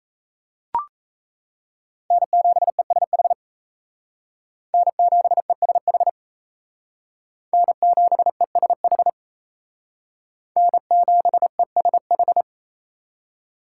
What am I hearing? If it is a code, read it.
Morse code, heard as a keyed tone, about 35 wpm, 700 Hz. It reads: N7ESH N7ESH N7ESH N7ESH
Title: Speed Racing - USA Call Signs 28wpm